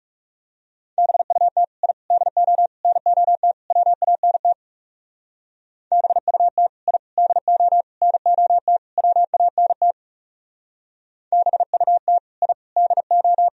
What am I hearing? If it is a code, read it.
Morse code, heard as a keyed tone, about 37 wpm, 700 Hz. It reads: BUT I DO NOT WANT BUT I DO NOT WANT BUT I DO
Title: Speed-Racing - Sets of 5 Words 30wpm